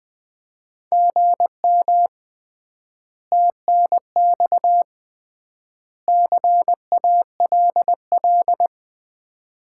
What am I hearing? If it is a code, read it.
Morse code, heard as a keyed tone, about 20 wpm, 700 Hz. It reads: GM TNX CALL
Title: Common QSO Phrases - 3x Word-Spacing - 20wpm